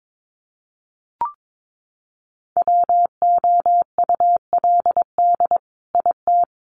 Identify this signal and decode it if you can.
Morse code, heard as a keyed tone, about 22 wpm, 700 Hz. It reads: WOULD IT
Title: Sentences from Top 800 Words 22wpm